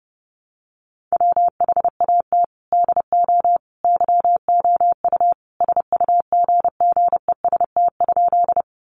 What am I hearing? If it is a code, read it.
Morse code, heard as a keyed tone, about 30 wpm, 700 Hz. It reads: WHAT DO YOU SUGGEST?
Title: Sentences from Top 1300 Words 30wpm